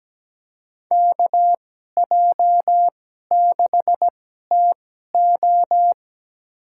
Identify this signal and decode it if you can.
Morse code, heard as a keyed tone, about 17 wpm, spaced as for 12 wpm, 700 Hz. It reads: KJ6TO
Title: USA Call Signs - 2x Character Spacing - 17wpm